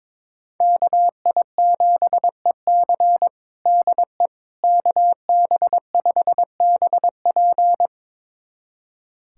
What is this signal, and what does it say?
Morse code, heard as a keyed tone, about 21 wpm, 700 Hz. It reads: KI7EC DE KB5BP
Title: USA Callsign DE Callsign - 5 Characters - 22wpm